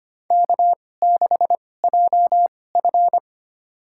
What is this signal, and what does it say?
Morse code, heard as a keyed tone, about 25 wpm, spaced as for 18 wpm, 700 Hz. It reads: K6JF
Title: USA Call Signs - 4 Characters - 2x Character Spacing - 25wpm